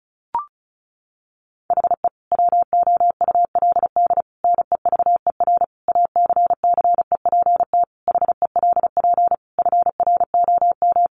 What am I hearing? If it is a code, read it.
Morse code, heard as a keyed tone, about 34 wpm, 700 Hz. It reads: HE WOULD NEVER ACCEPT HELP FROM
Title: Sentences from Top 1300 Words 35wpm